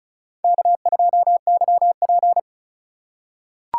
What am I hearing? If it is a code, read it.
Morse code, heard as a keyed tone, about 36 wpm, 700 Hz. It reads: K2YP
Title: USA Call Signs - 4 Characters - 4x Character Spacing - 35wpm